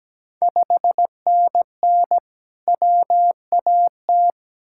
Morse code, heard as a keyed tone, about 17 wpm, 700 Hz. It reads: 5NN WAT
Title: Ontario QSO Party Exchange 17wpm